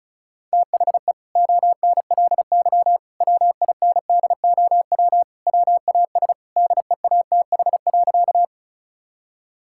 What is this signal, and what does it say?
Morse code, heard as a keyed tone, about 34 wpm, 700 Hz. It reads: THE ONLY WINDOW WAS DEATH.
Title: Sentences from Top 1600 Words 35wpm